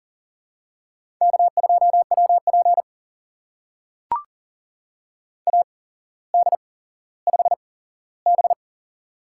Morse code, heard as a keyed tone, about 42 wpm, spaced as for 17 wpm, 700 Hz. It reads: K2WP AD5B
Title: USA Call Signs - 4 Characters - 8x Character Spacing - 40wpm